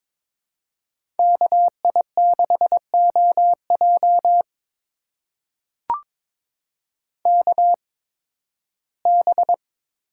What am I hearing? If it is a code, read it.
Morse code, heard as a keyed tone, about 22 wpm, spaced as for 14 wpm, 700 Hz. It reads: KI6OJ KB
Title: USA Call Signs - 5 Characters - 8x Character Spacing - 22wpm